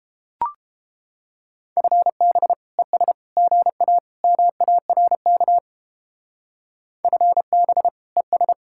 Morse code, heard as a keyed tone, about 29 wpm, 700 Hz. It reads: FB ES GA MARK FB ES